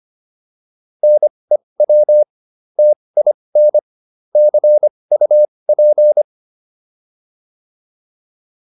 Morse code, heard as a keyed tone, about 25 wpm, spaced as for 20 wpm, 600 Hz. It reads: NEW TIN CUP